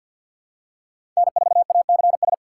Morse code, heard as a keyed tone, about 52 wpm, 700 Hz. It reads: N4ACS
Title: USA Call Signs - 5 Characters - 3x Character Spacing - 50wpm